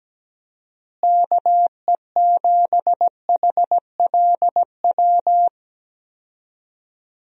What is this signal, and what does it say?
Morse code, heard as a keyed tone, about 17 wpm, 700 Hz. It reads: KE7HLW
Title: USA Call Signs - 6 Characters - Repeated 2x - 17wpm